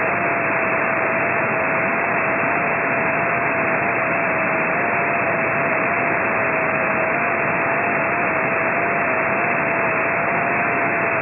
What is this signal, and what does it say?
FT8.
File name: ft8-vs-cw / 60-to-150 FT8 -9